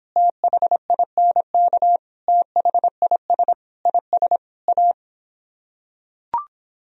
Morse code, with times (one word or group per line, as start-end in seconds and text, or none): 0.2-2.0 THINK
2.3-3.5 THIS
3.9-4.4 IS
4.7-4.9 A